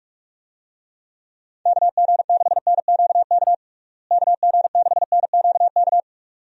Morse code, heard as a keyed tone, about 45 wpm, 700 Hz. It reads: KG6NQK KG6NQK
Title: USA Call Signs - 6 Characters - Repeated 2x - 45wpm